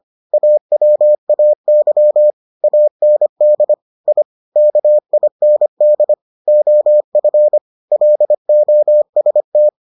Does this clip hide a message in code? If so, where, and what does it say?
Yes: Morse code from 0.3 to 9.7 s, heard as a keyed tone, about 25 wpm, 600 Hz. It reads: AWAY AND I KIND OF LOST